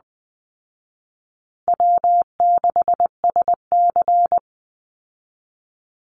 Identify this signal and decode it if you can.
Morse code, heard as a keyed tone, about 20 wpm, 700 Hz. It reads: W6SC
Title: Easy Call Signs Repeated 3x - 20wpm